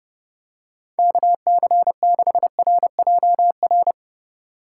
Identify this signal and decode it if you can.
Morse code, heard as a keyed tone, about 29 wpm, 700 Hz. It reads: KC6RJR
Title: Speed Racing - USA 2x3 Call Signs 25wpm